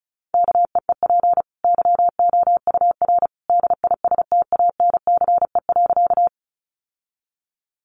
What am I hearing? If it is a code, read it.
Morse code, heard as a keyed tone, about 34 wpm, 700 Hz. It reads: KEEP YOUR DISTANCE.
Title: Sentences from Top 1200 Words 35wpm